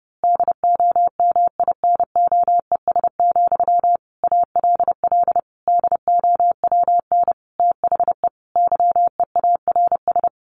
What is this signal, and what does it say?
Morse code, heard as a keyed tone, about 30 wpm, 700 Hz. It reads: DOMINOES, ALL DOWN THE YEARS
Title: A Sound Of Thunder 30wpm